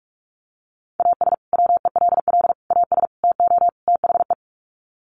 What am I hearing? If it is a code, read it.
Morse code, heard as a keyed tone, about 44 wpm, 700 Hz. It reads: AS WELL AS TO THE
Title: Sets of 5 Words Encore2 45wpm